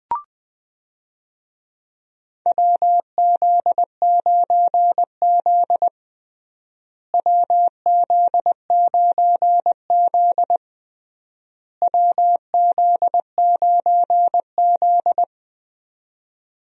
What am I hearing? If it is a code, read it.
Morse code, heard as a keyed tone, about 20 wpm, 700 Hz. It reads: WZ9Z WZ9Z WZ9Z
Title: USA Call Signs - 4 Characters - Repeated 3x - 20wpm